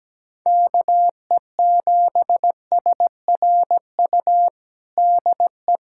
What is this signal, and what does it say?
Morse code, heard as a keyed tone, about 17 wpm, 700 Hz. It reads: KE7SRU DE
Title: USA Callsign DE Callsign 17wpm